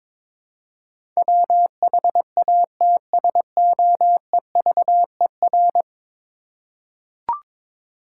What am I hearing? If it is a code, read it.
Morse code, heard as a keyed tone, about 22 wpm, 700 Hz. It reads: WHATSOEVER
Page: morse-code-ninja-practice-22wpm